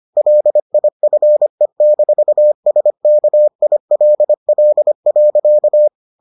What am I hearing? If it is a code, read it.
Morse code, heard as a keyed tone, about 25 wpm, 600 Hz. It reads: LIFE-SKILL.